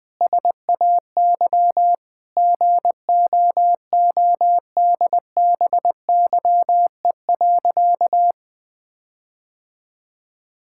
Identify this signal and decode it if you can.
Morse code, heard as a keyed tone, about 20 wpm, 700 Hz. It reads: SAY GOODBYE.
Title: Sentences with 7-Word Phrases - Spoken Beginning - 20wpm